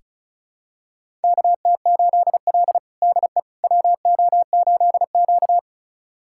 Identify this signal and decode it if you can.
Morse code, heard as a keyed tone, about 36 wpm, 700 Hz. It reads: KT8L DE WO8Q
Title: USA Callsign DE Callsign - 4 Characters - 35wpm